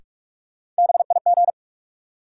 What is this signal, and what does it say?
Morse code, heard as a keyed tone, about 43 wpm, 700 Hz. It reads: BIG
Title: Sentences from Top 200 Words - New Word Drill - 45wpm